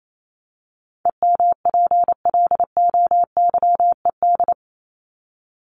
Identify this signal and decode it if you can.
Morse code, heard as a keyed tone, about 28 wpm, 700 Hz. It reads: EMPLOYED
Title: ED Suffix - Top 100 Words with Base Words 28wpm